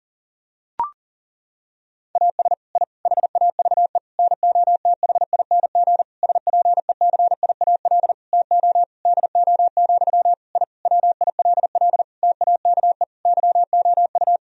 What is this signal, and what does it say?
Morse code, heard as a keyed tone, about 40 wpm, 700 Hz. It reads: AS I HAVE NOTHING SPECIAL TO DO, I WILL TAKE YOU
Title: Sentences from Top 1000 Words - Review Part 3 - 40wpm